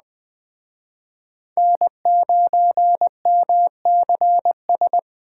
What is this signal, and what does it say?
Morse code, heard as a keyed tone, about 21 wpm, 700 Hz. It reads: N9MCS
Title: Speed Racing - USA 1x3 Call Signs 17wpm